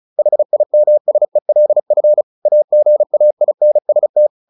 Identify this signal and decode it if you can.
Morse code, heard as a keyed tone, about 34 wpm, 600 Hz. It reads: HIMSELF AGAINST
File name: the old man and the sea / the old man and the sea 1 35wpm